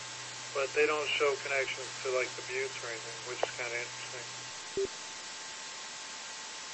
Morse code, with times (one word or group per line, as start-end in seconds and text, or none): none